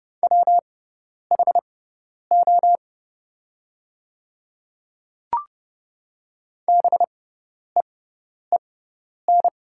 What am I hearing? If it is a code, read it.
Morse code, heard as a keyed tone, about 30 wpm, spaced as for 10 wpm, 700 Hz. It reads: WHO BEEN
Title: 6x Character Spacing - Top 100 Words - 30wpm